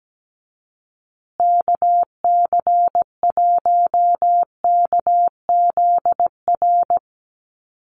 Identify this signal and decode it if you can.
Morse code, heard as a keyed tone, about 17 wpm, 700 Hz. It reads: KC1KZR
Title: USA Call Signs Repeated 2x 17wpm